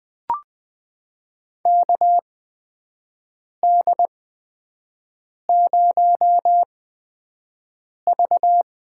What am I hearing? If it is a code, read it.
Morse code, heard as a keyed tone, about 20 wpm, spaced as for 5 wpm, 700 Hz. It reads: KD0V
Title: USA Call Signs - 6 Characters - 8x Character Spacing - 20wpm